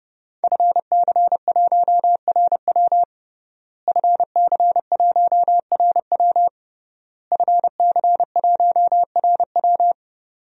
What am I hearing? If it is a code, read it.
Morse code, heard as a keyed tone, about 32 wpm, 700 Hz. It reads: FC1RW FC1RW FC1RW
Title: DXCC-100 Call Signs - 5 Characters - Repeated 3x - 30wpm